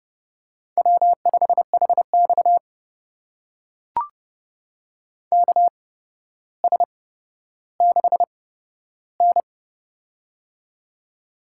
Morse code, heard as a keyed tone, about 31 wpm, spaced as for 13 wpm, 700 Hz. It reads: W5HX KS6N
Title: USA Call Signs - 8x Character Spacing - 30wpm